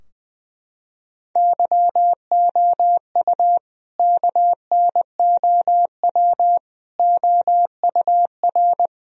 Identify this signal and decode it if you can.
Morse code, heard as a keyed tone, about 21 wpm, 700 Hz. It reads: YOU KNOW OUR
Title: Sentences from Top 1000 Words - Review Part 4 - 20wpm